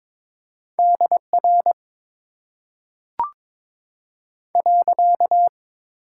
Morse code, heard as a keyed tone, about 22 wpm, 700 Hz. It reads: D R .
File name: Top 90 QSO Elements - 8x Character Spacing - 22wpm